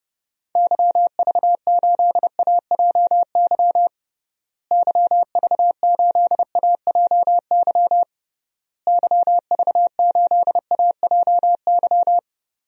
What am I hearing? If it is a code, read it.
Morse code, heard as a keyed tone, about 32 wpm, 700 Hz. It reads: YV8AJY YV8AJY YV8AJY
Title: DXCC-100 Call Signs - 6 Characters - Repeated 3x - 30wpm